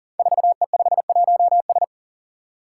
Morse code, heard as a keyed tone, about 41 wpm, 700 Hz. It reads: 4E51S
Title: Five Challenging Letter-Number Combinations 40wpm